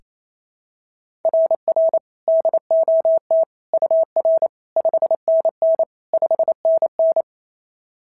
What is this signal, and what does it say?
Morse code, heard as a keyed tone, about 28 wpm, 690 Hz. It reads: RR DOT UR 5NN 5NN
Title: SOTA QSO Practice 28wpm